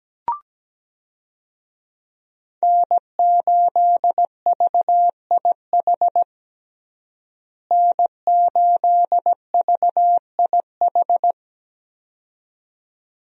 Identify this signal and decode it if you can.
Morse code, heard as a keyed tone, about 17 wpm, 700 Hz. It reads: N8VIH N8VIH
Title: USA Call Signs - 5 Characters - Repeated 2x - 17wpm